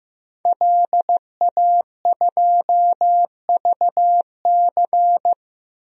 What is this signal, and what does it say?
Morse code, heard as a keyed tone, about 15 wpm, 700 Hz. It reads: LA2VC
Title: DXCC-100 Call Signs - 5 Characters - 15wpm